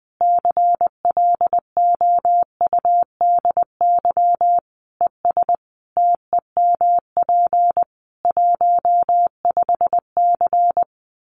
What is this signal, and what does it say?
Morse code, heard as a keyed tone, about 20 wpm, 700 Hz. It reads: CLOUDY ES TEMP 15C
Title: Common QSO Phrases 20wpm